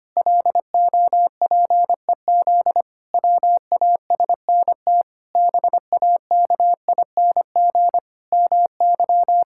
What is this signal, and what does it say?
Morse code, heard as a keyed tone, about 25 wpm, 700 Hz. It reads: LOPEZ WASNT BAKING MY
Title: Pangram Challenge v2 - 25wpm